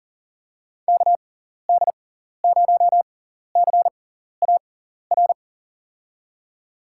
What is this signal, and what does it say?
Morse code, heard as a keyed tone, about 40 wpm, spaced as for 14 wpm, 700 Hz. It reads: KD0CAR
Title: USA Call Signs - 6 Characters - 6x Character Spacing - 40wpm